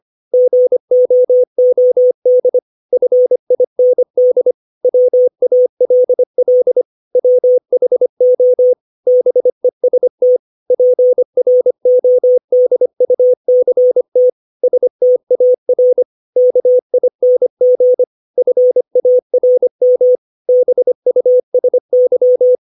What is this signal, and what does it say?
Morse code, heard as a keyed tone, about 25 wpm, 500 Hz. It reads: GOOD FIND WALL WHO BEST PRODUCT STAR KING FARM BUSY
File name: 632 words / 632 Words 25WPM